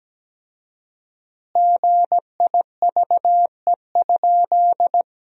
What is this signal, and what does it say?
Morse code, heard as a keyed tone, about 17 wpm, 700 Hz. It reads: GIVE?
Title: 4x Word-Spacing - Sentences from Top 200 Words - 17wpm